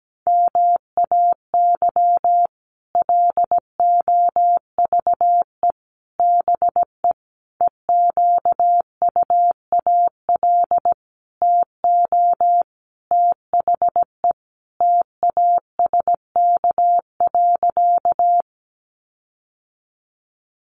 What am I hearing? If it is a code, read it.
Morse code, heard as a keyed tone, about 17 wpm, 700 Hz. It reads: MAY LOVE BE EQUAL TO THE TASK.
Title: Sentences from Top 1400 Words 17wpm